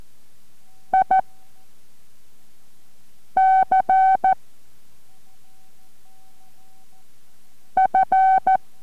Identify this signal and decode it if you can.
Morse code, heard as a keyed tone, about 14 wpm, 760 Hz. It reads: I C F